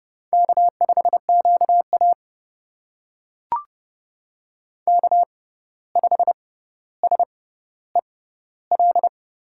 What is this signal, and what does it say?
Morse code, heard as a keyed tone, about 30 wpm, spaced as for 14 wpm, 700 Hz. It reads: K5QA K5SEL
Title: USA Call Signs - 6x Character Spacing - 30wpm